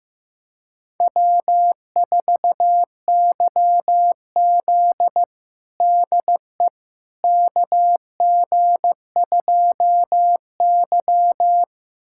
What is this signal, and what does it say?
Morse code, heard as a keyed tone, about 15 wpm, 700 Hz. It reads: W4YZ DE KG2Y
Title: USA Callsign DE Callsign - 4 Characters - 15wpm